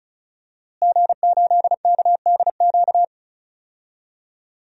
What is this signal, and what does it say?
Morse code, heard as a keyed tone, about 34 wpm, 700 Hz. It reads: G8KDQ
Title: DXCC-100 Call Signs - 5 Characters - Repeated 2x - 35wpm